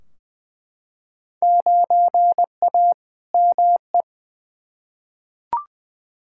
Morse code, heard as a keyed tone, about 20 wpm, 700 Hz. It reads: 9A ME
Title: Field Day Exchanges Encore - 20wpm